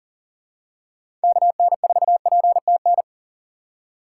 Morse code, heard as a keyed tone, about 40 wpm, 700 Hz. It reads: KN4PTN
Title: USA Call Signs - 6 Characters - 4x Character Spacing - 40wpm